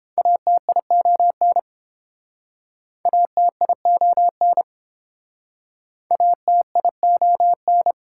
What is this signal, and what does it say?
Morse code, heard as a keyed tone, about 31 wpm, 700 Hz. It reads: ATION ATION ATION